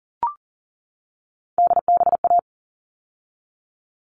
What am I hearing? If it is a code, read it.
Morse code, heard as a keyed tone, about 40 wpm, 700 Hz. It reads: DBA
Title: All Cut Numbers - 3-Digit 40wpm